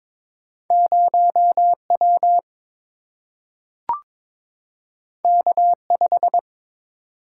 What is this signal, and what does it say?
Morse code, heard as a keyed tone, about 22 wpm, 700 Hz. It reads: 0W K5
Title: Single-Letter Single-Number 22wpm